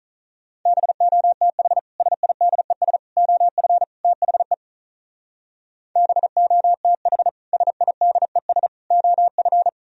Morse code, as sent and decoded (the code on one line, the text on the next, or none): -... --- - .... / ... .. -.. ... / --- ..-. / - .... / -... --- - .... / ... .. -.. ... / --- ..-.
BOTH SIDES OF THE BOTH SIDES OF